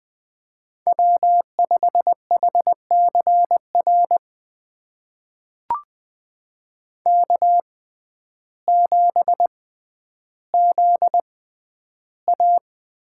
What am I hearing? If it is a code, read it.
Morse code, heard as a keyed tone, about 20 wpm, spaced as for 11 wpm, 700 Hz. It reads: W5HCR K7ZA
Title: USA Call Signs - 5 Characters - 6x Character Spacing - 20wpm